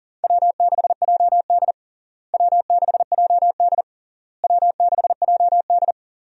Morse code, heard as a keyed tone, about 37 wpm, 700 Hz. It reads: W6JD W6JD W6JD
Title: USA Call Signs - 4 Characters - Repeated 3x - 40wpm